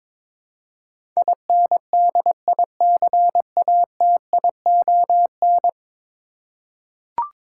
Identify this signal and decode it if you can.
Morse code, heard as a keyed tone, about 23 wpm, 700 Hz. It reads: INDICATION
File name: practice / 22wpm / 10 Letter Words - First 3 Letters Given - 22wpm